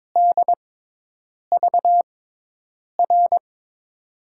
Morse code, heard as a keyed tone, about 22 wpm, spaced as for 8 wpm, 700 Hz. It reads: DVR
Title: USA Call Signs - 5 Characters - 6x Character Spacing - 22wpm